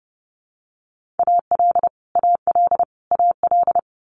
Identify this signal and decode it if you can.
Morse code, heard as a keyed tone, about 30 wpm, 700 Hz. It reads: AL AL AL
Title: Multi-letter Phonemes Repeated 3x 30wpm